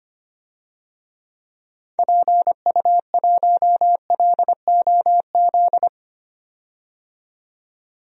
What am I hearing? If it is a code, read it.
Morse code, heard as a keyed tone, about 25 wpm, 700 Hz. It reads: PU1LOZ
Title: DXCC-100 Call Signs - 6 Characters - 25wpm